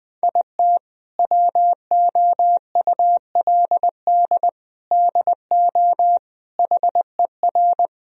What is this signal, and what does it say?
Morse code, heard as a keyed tone, about 20 wpm, 700 Hz. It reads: IT WOULD DO HER
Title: Sentences from Top 1000 Words - Review Part 3 - 20wpm